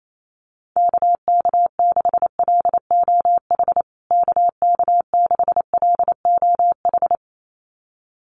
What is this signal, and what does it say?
Morse code, heard as a keyed tone, about 28 wpm, 700 Hz. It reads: KK6LOH KK6LOH
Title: USA Call Signs Repeated 2x 28wpm